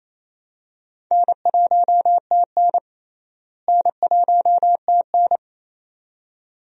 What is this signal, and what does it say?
Morse code, heard as a keyed tone, about 28 wpm, 700 Hz. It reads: N1TN N1TN
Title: USA Call Signs - 4 Characters - Repeated 2x - 28wpm